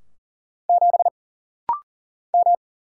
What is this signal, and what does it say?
Morse code, heard as a keyed tone, about 40 wpm, 700 Hz. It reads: 7 M
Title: Single Letter-Number Alt - Rapid-Fire 40wpm